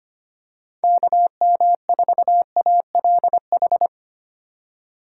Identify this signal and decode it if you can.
Morse code, heard as a keyed tone, about 25 wpm, 700 Hz. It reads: KM4ALH
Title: USA Call Signs - 6 Characters - 2x Character Spacing - 25wpm